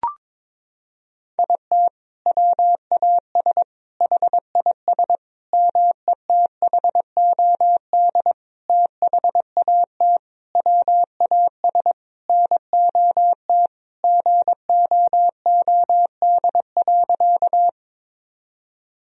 Morse code, heard as a keyed tone, about 22 wpm, 700 Hz. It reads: IT WAS HIS METHOD THAT WAS NOT GOOD.
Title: Sentences from Top 1000 Words - Review Part 3 - 22wpm